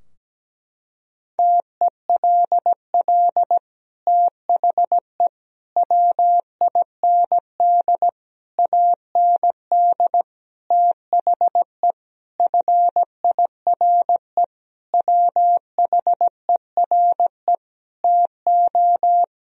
Morse code, heard as a keyed tone, about 17 wpm, 700 Hz. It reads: TELL THE WIND AND THE FIRE WHERE TO
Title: Sentences from Top 1000 Words - Review Part 2 - 17wpm